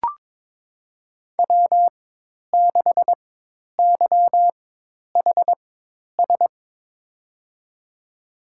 Morse code, heard as a keyed tone, about 22 wpm, spaced as for 10 wpm, 700 Hz. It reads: W6YHS
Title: USA Call Signs - 5 Characters - 4x Character Spacing - 22wpm